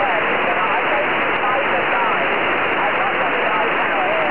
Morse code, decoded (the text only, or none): none